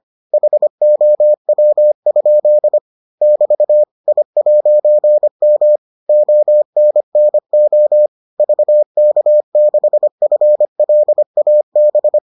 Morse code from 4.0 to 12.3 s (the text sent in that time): I'M ONNO VK6FLAB